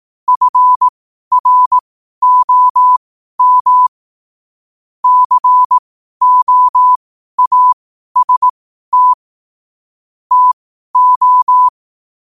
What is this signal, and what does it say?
Morse code, heard as a keyed tone, about 18 wpm, spaced as for 12 wpm, 1000 Hz. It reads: FROM COAST TO
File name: qotdfiles / 12wpm